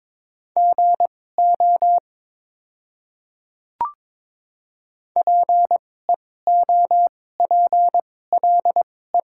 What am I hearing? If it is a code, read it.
Morse code, heard as a keyed tone, about 22 wpm, spaced as for 16 wpm, 690 Hz. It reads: GO PEOPLE
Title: 2x Character Spacing - Top 100 Words - 22wpm